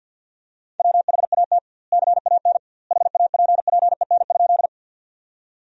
Morse code, heard as a keyed tone, about 50 wpm, 700 Hz. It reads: WHAT CAN HAPPEN?
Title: 2x Word-Spacing - Sentences from Top 400 Words - 50wpm